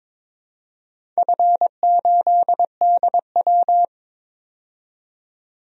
Morse code, heard as a keyed tone, about 22 wpm, 700 Hz. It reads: F8DW
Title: DXCC-100 Call Signs - 4 Characters - 22wpm